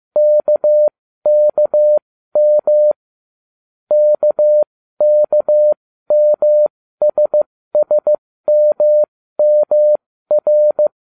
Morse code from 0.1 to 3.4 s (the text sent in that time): KKM